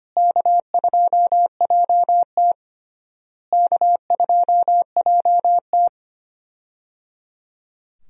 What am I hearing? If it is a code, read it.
Morse code, heard as a keyed tone, about 25 wpm, 700 Hz. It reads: K2JT K2JT
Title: USA Call Signs - 4 Characters - Repeated 3x - 25wpm